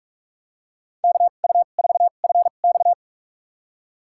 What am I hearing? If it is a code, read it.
Morse code, heard as a keyed tone, about 44 wpm, spaced as for 32 wpm, 700 Hz. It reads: KU4FX
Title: USA Call Signs - 5 Characters - 2x Character Spacing - 45wpm